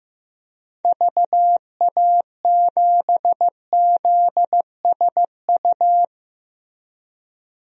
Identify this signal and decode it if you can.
Morse code, heard as a keyed tone, about 15 wpm, 700 Hz. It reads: VA7ZSU
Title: DXCC-100 Call Signs - 6 Characters - Repeated 2x - 15wpm